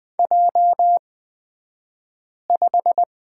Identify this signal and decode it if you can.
Morse code, heard as a keyed tone, about 20 wpm, spaced as for 5 wpm, 700 Hz. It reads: J5
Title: 254 Lesson 41d - Period - Two Characters 20wpm